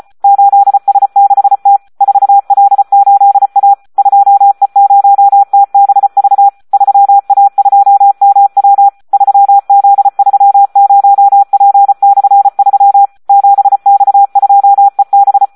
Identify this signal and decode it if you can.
Morse code, heard as a keyed tone, about 34 wpm, 800 Hz. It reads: 8S6T 4L8A 2E0TBV 3A2MW 3Z30P/3 7X2EB
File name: Audio practica CW / SD 1 34